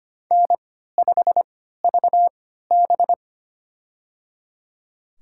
Morse code, heard as a keyed tone, about 25 wpm, spaced as for 14 wpm, 700 Hz. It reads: N5VB
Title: USA Call Signs - 4 Characters - 3x Character Spacing - 25wpm